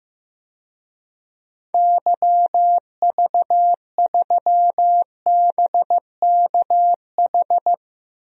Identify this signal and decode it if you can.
Morse code, heard as a keyed tone, about 15 wpm, 700 Hz. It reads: YV3BKH
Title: DXCC-100 Call Signs - 6 Characters - 15wpm